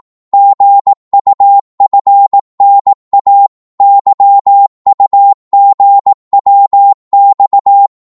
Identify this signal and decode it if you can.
Morse code, heard as a keyed tone, about 18 wpm, 800 Hz. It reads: GUFNA YUGWX
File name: Morse / ltr-18wpm